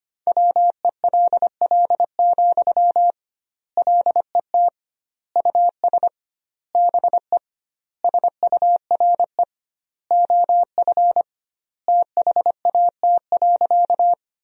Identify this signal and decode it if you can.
Morse code, heard as a keyed tone, about 25 wpm, 700 Hz. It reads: WELL, LET US BE SURE OF THAT.
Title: 2x Word-Spacing - Sentences from Top 300 Words - 25wpm